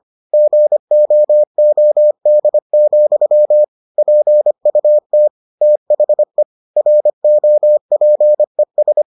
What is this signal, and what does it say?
Morse code, heard as a keyed tone, about 25 wpm, 600 Hz. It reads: GOOD, PUT THE ROPES